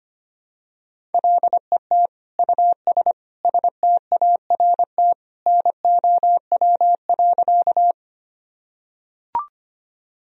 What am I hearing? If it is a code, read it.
Morse code, heard as a keyed tone, about 25 wpm, 700 Hz. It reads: LET US START NOW.
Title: Sentences from Top 1000 Words - Review Part 1 - 25wpm